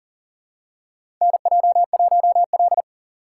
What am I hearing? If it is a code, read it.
Morse code, heard as a keyed tone, about 38 wpm, 700 Hz. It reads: NJ1L
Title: USA Call Signs - 4 Characters - 3x Character Spacing - 40wpm